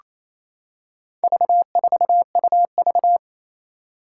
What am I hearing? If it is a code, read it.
Morse code, heard as a keyed tone, about 28 wpm, 700 Hz. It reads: V4UV
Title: Four Challenging Letter-Number Combinations 28wpm